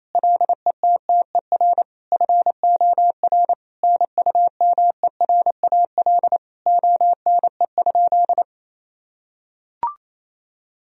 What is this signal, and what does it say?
Morse code, heard as a keyed tone, about 28 wpm, 700 Hz. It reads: LETTER FOR NUMERAL ONE?
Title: Sentences from Top 1000 Words - Review Part 2 - 28wpm